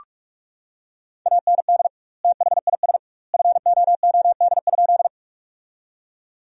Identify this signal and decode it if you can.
Morse code, heard as a keyed tone, about 47 wpm, 700 Hz. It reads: AND THIS FOOD?
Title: 2x Word-Spacing - Sentences from Top 300 Words - 45wpm